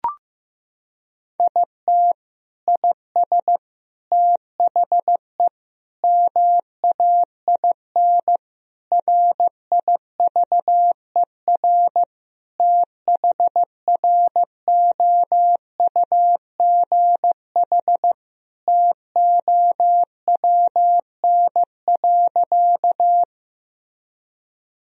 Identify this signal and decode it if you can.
Morse code, heard as a keyed tone, about 16 wpm, 700 Hz. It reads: IT IS THE MAIN RIVER THROUGH TOWN.
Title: Sentences from Top 1000 Words - Review Part 2 - 15wpm